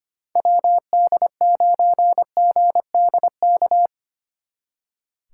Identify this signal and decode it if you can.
Morse code, heard as a keyed tone, about 31 wpm, 700 Hz. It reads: WD9GDK
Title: Speed Racing - USA 2x3 Call Signs 25wpm